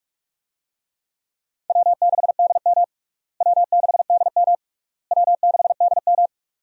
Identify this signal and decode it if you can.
Morse code, heard as a keyed tone, about 47 wpm, 700 Hz. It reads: W6DM W6DM W6DM
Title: USA Call Signs - 4 Characters - Repeated 3x - 45wpm